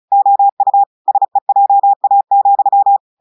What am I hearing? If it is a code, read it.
Morse code, heard as a keyed tone, about 36 wpm, 800 Hz. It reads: OU SEJA,
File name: portugais texte en clair 2 vitesse 35 mots minutes